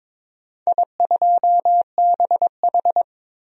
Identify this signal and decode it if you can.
Morse code, heard as a keyed tone, about 23 wpm, 700 Hz. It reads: I2BH
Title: DXCC-100 Call Signs - 4 Characters - 22wpm